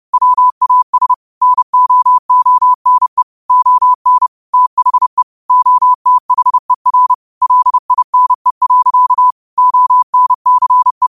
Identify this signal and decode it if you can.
Morse code, heard as a keyed tone, about 30 wpm, 1000 Hz. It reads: WAS NO ONE ON THE OTHER LINE. ONCE